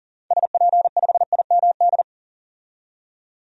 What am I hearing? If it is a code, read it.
Morse code, heard as a keyed tone, about 40 wpm, 700 Hz. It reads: SP5IMD
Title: DXCC-100 Call Signs - 6 Characters - 40wpm